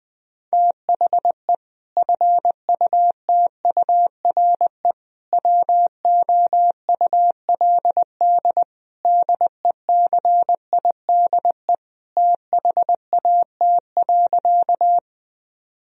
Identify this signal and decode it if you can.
Morse code, heard as a keyed tone, about 21 wpm, 700 Hz. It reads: THE FUTURE WOULD DECIDE THAT.
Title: Sentences from Top 1000 Words - Review Part 3 - 20wpm